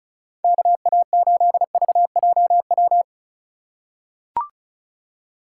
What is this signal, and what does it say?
Morse code, heard as a keyed tone, about 36 wpm, 700 Hz. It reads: KA8VJW
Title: USA Call Signs - 3x Character Spacing - 35wpm